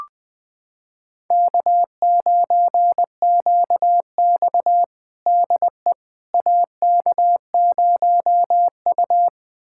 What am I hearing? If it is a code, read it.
Morse code, heard as a keyed tone, about 21 wpm, 700 Hz. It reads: K9QX DE AK0U
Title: USA Callsign DE Callsign - 4 Characters - 20wpm